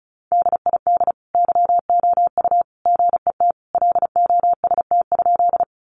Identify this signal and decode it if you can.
Morse code, heard as a keyed tone, about 34 wpm, 700 Hz. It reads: DID YOU GET LOST?
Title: Sentences from Top 1400 Words 35wpm